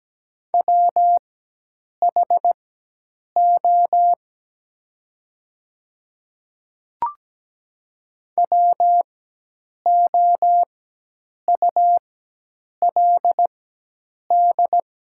Morse code, heard as a keyed tone, about 17 wpm, spaced as for 8 wpm, 700 Hz. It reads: WHO WOULD
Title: 4x Character Spacing - Top 100 Words - 17wpm